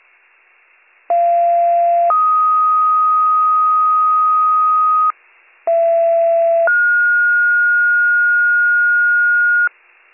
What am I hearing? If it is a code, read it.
Two-tone page, heard as a one-second tone, then a three-second tone, about 690 then 1250 Hz, about 670 then 1430 Hz. It is heard from 1.1 to 9.7 s.